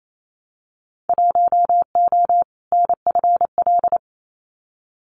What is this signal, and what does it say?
Morse code, heard as a keyed tone, about 28 wpm, 700 Hz. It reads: 1O NFL
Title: Winter Field Day Exchanges 28wpm